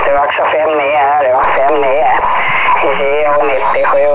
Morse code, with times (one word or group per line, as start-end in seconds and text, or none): none